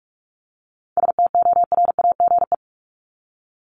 Morse code, heard as a keyed tone, about 44 wpm, 700 Hz. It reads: STORAGE
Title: Top 1251-1500 Words 45wpm